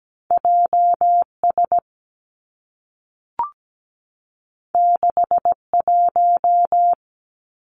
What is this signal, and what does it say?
Morse code, heard as a keyed tone, about 17 wpm, 700 Hz. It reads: J S 6 1
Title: Two Challenging Letter-Number Combinations 17wpm